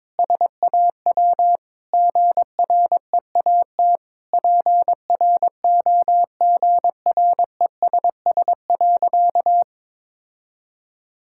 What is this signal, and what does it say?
Morse code, heard as a keyed tone, about 22 wpm, 700 Hz. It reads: SAW GREAT PROGRESS.
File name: Sentences with 7-Word Phrases - 22wpm